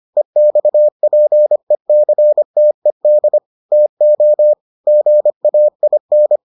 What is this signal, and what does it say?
Morse code, heard as a keyed tone, about 25 wpm, 600 Hz. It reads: EXPECTED TO GAIN